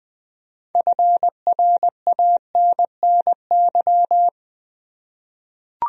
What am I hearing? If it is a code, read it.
Morse code, heard as a keyed tone, about 20 wpm, 700 Hz. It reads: FRANNY
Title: USA Common Names - 20wpm